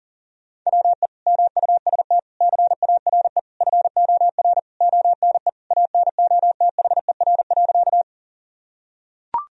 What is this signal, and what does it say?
Morse code, heard as a keyed tone, about 40 wpm, 700 Hz. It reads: WE MUST CARE FOR ONE ANOTHER.
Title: Sentences from Top 1000 Words - Review Part 1 - 40wpm